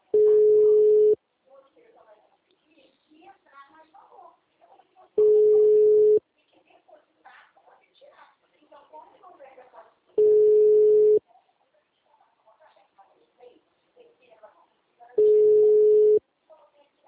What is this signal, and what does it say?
Ringback tone (Europe, 425 Hz), with speech.